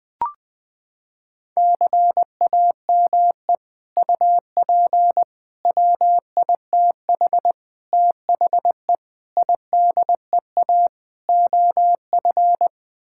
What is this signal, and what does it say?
Morse code, heard as a keyed tone, about 20 wpm, 700 Hz. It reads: CAME UP WITH THE IDEA OF
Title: Sets of 6 Words - 20wpm